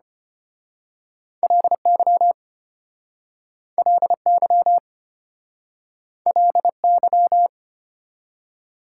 Morse code, heard as a keyed tone, about 33 wpm, 700 Hz. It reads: LY LY LY